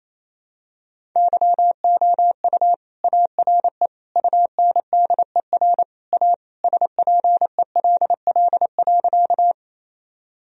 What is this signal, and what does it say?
Morse code, heard as a keyed tone, about 28 wpm, 700 Hz. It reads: YOU ARE UNDER A SPELL.
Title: Sentences from Top 1000 Words - Review Part 1 - 28wpm